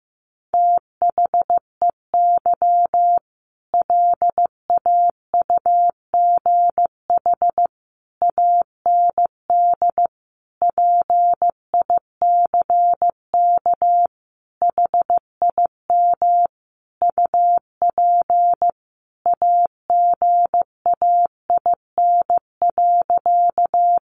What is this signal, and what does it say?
Morse code, heard as a keyed tone, about 15 wpm, 700 Hz. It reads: THEY LAUGH AND PICK HIM UP AGAIN.
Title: Sentences from Top 1000 Words 15wpm